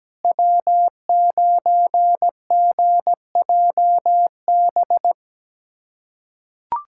Morse code, heard as a keyed tone, about 17 wpm, 700 Hz. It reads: W9GJB
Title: USA Callsign DE Callsign - 5 Characters - 17wpm